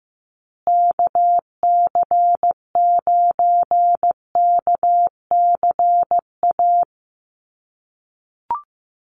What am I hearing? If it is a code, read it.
Morse code, heard as a keyed tone, about 15 wpm, 700 Hz. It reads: KC9KCA
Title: USA Call Signs Repeated 3x 15wpm